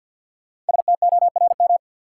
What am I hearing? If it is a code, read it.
Morse code, heard as a keyed tone, about 52 wpm, 700 Hz. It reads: STORM
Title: Sentences from Top 1500 Words - New Word Drill Alt - 50wpm